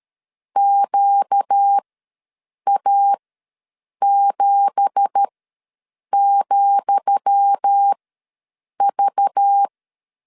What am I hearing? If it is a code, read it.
Morse code, heard as a keyed tone, about 13 wpm, spaced as for 7 wpm, 800 Hz. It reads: QA7,V
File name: LEZIONI / 10 SIGNS / 10D COMA ws 12wpm 3618